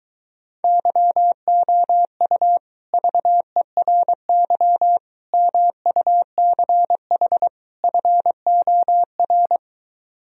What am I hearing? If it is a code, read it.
Morse code, heard as a keyed tone, about 23 wpm, 700 Hz. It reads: YOU VERY MUCH FOR